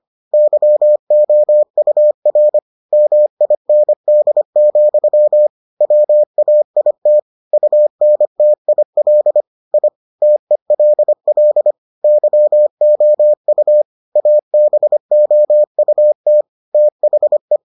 Morse code, heard as a keyed tone, about 25 wpm, 600 Hz. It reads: YOUR MIND, WAIT UNTIL I TELL YOU ABOUT THE